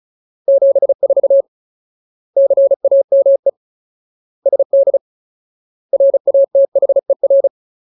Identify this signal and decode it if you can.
Morse code, heard as a keyed tone, about 34 wpm, 550 Hz. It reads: 74 CAME SD RATHER
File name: Ultimate Word List / 35WPM Ultimate